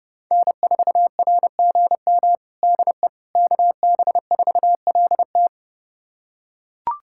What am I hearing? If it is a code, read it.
Morse code, heard as a keyed tone, about 32 wpm, 700 Hz. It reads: N4RGM DE KB4LT
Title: USA Callsign DE Callsign - 5 Characters - 30wpm